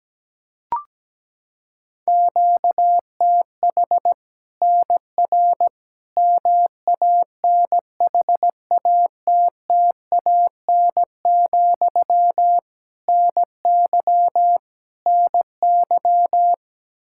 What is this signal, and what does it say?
Morse code, heard as a keyed tone, about 17 wpm, 700 Hz. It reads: QTH NR MANHATTAN, NY NY
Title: Practice QSO - Exchange 1 - Beginning - 17wpm